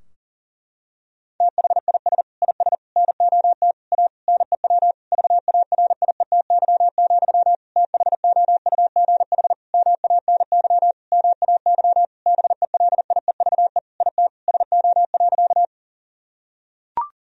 Morse code, heard as a keyed tone, about 40 wpm, 700 Hz. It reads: THIS IS NOT A NEW VARIETY, THOUGH MANY MAY BELIEVE IT SO.
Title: Sentences from Top 1000 Words - Review Part 4 - 40wpm